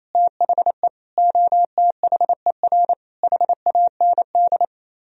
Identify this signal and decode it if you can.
Morse code, heard as a keyed tone, about 35 wpm, 700 Hz. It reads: THE OTHER HAND